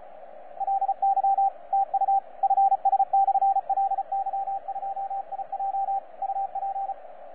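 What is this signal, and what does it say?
Morse code, heard as a keyed tone, about 36 wpm, 780 Hz, with radio noise. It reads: LY TU FS/LY2IJ UP
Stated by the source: FS/LJ2IJ heard on 12 metres CW, from DX World.net.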